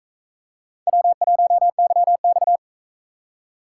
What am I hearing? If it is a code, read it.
Morse code, heard as a keyed tone, about 39 wpm, 700 Hz. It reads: W1YX